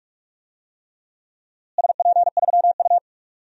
Morse code, heard as a keyed tone, about 42 wpm, 700 Hz. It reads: SP3U
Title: DXCC-100 Call Signs - 4 Characters - Repeated 3x - 45wpm